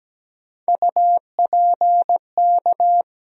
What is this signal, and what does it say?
Morse code, heard as a keyed tone, about 17 wpm, 700 Hz. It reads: UPK
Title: USA Call Sign Postfixes - 3 Letters 17wpm